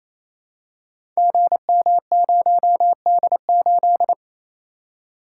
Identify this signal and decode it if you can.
Morse code, heard as a keyed tone, about 27 wpm, 700 Hz. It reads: GM0D8
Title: Code Groups - 5 Letter:Numbers - 28wpm